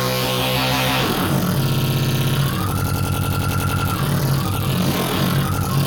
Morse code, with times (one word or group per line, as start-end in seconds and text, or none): none